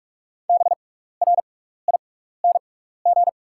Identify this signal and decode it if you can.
Morse code, heard as a keyed tone, about 48 wpm, spaced as for 16 wpm, 700 Hz. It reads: BRING
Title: 6x Character Spacing - Top 500 Words - 45wpm